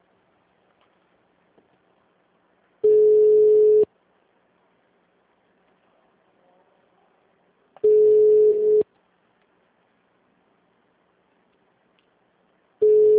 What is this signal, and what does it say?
Ringback tone (Europe, 425 Hz).
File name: out- -2012- - - .2352